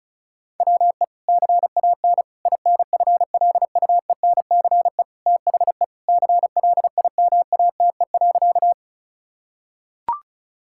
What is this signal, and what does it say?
Morse code, heard as a keyed tone, about 34 wpm, 700 Hz. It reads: WE CAN INFLUENCE THE CLIMATE.
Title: Sentences from Top 1600 Words 35wpm